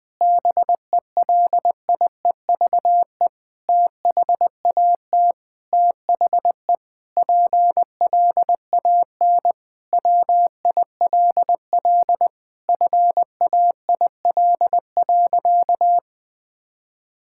Morse code, heard as a keyed tone, about 21 wpm, 700 Hz. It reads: BELIEVE THAT THE PLAN WILL FAIL.
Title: Sentences with 7-Word Phrases - 20wpm